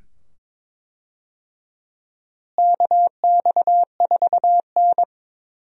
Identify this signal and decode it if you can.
Morse code, heard as a keyed tone, about 21 wpm, 700 Hz. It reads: KX4N
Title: USA Call Signs - 4 Characters - Repeated 2x - 22wpm